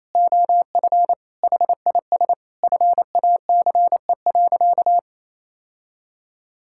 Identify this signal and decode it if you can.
Morse code, heard as a keyed tone, about 28 wpm, 700 Hz. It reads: OF HIS FACE.
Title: Sentences from Top 1000 Words - Review Part 5 - 28wpm